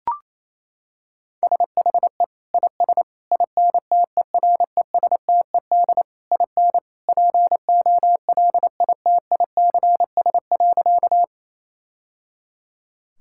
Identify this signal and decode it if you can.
Morse code, heard as a keyed tone, about 28 wpm, 700 Hz. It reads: SHE IS INTERESTED IN POLITICS.
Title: Sentences from Top 1000 Words - Review Part 6 - 28wpm